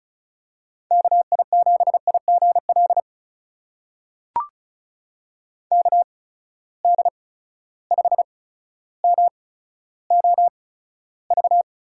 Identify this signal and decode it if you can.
Morse code, heard as a keyed tone, about 36 wpm, spaced as for 15 wpm, 700 Hz. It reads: KI7IGL KD5MOV
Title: USA Call Signs - 6 Characters - 8x Character Spacing - 35wpm